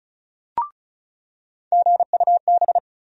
Morse code, heard as a keyed tone, about 36 wpm, 700 Hz. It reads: GUB
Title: USA Call Sign Postfixes - 3 Letters 35wpm